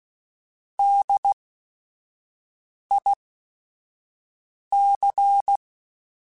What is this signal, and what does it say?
Morse code, heard as a keyed tone, about 16 wpm, spaced as for 5 wpm, 800 Hz. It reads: DIC
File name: Technical Subjects / Technical Subjects-mp3s / XL @5